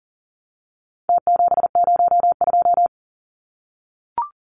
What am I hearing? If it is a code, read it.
Morse code, heard as a keyed tone, about 39 wpm, 700 Hz. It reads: T702
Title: Numbers - Contest Style - 4-Digit 40wpm